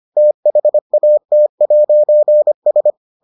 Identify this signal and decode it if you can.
Morse code, heard as a keyed tone, about 25 wpm, 600 Hz. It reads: THAT'S